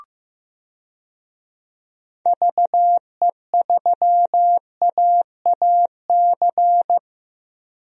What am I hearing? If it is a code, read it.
Morse code, heard as a keyed tone, about 15 wpm, 700 Hz. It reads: VE3AAC
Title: DXCC-100 Call Signs - 6 Characters - Repeated 3x - 15wpm